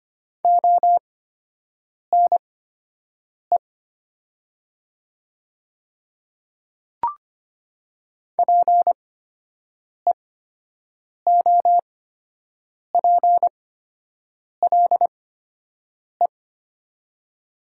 Morse code, heard as a keyed tone, about 25 wpm, spaced as for 7 wpm, 700 Hz. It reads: ONE PEOPLE
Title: 8x Character Spacing - Top 100 Words - 25wpm